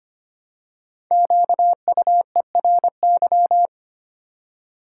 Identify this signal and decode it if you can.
Morse code, heard as a keyed tone, about 25 wpm, 700 Hz. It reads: QUERY
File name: 5 Letter Words - First 3 Letters Given - 25wpm